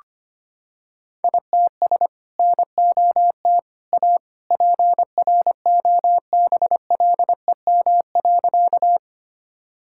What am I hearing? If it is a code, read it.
Morse code, heard as a keyed tone, about 25 wpm, 700 Hz. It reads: ITS NOT A PROBLEM.
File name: practice / 25wpm / Sentences with 2-Word Phrases - 25wpm